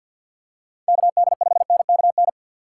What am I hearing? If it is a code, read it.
Morse code, heard as a keyed tone, about 53 wpm, 700 Hz. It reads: KD5NKN
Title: USA Call Signs - 6 Characters - 6x Character Spacing - 50wpm